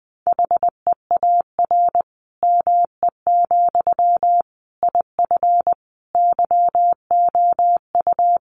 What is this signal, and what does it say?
Morse code, heard as a keyed tone, about 20 wpm, 700 Hz. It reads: HEAR ME, IF YOU
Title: Sentences from Top 1500 Words 20wpm